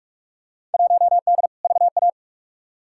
Morse code, heard as a keyed tone, about 45 wpm, 700 Hz. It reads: 1D VA
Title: Field Day Exchanges - 45wpm